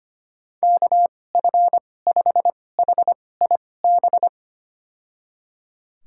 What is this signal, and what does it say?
Morse code, heard as a keyed tone, about 25 wpm, spaced as for 18 wpm, 700 Hz. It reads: KF5HIB
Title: USA Call Signs - 2x Character Spacing - 25wpm